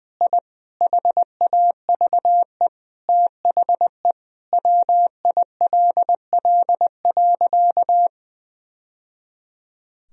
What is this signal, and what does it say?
Morse code, heard as a keyed tone, about 20 wpm, 700 Hz. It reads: I HAVE THE WILL.
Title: Sentences from Top 1000 Words - Review Part 2 - 20wpm